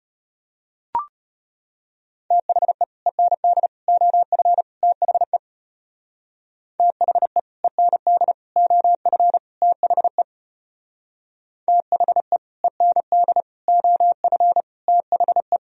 Morse code, heard as a keyed tone, about 34 wpm, 700 Hz. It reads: THE END OF THE THE END OF THE THE END OF THE